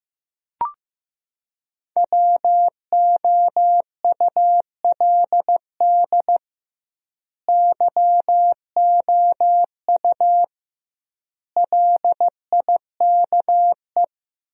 Morse code, heard as a keyed tone, about 16 wpm, 700 Hz. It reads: WOULD YOU LIKE
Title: 2x Word-Spacing - Sentences from Top 100 Words - 15wpm